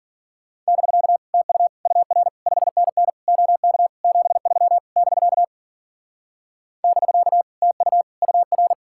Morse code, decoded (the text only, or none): <BK> TU UR 5NN OK 73 <BK> <BK> TU UR